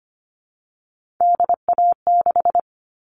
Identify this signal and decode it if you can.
Morse code, heard as a keyed tone, about 25 wpm, 700 Hz. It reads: DA6
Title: All Cut Numbers - 3-Digit 25wpm